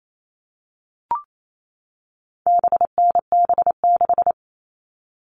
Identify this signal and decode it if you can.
Morse code, heard as a keyed tone, about 28 wpm, 700 Hz. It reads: BNB6
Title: All Cut Numbers - 4-Digit 28wpm